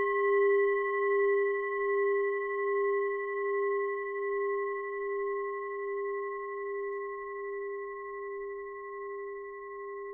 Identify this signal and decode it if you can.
Dial tone (Japan), 400 Hz, with noise.